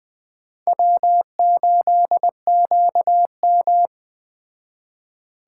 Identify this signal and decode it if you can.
Morse code, heard as a keyed tone, about 20 wpm, 700 Hz. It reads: W8QM
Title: Speed Racing - USA Call Signs 17wpm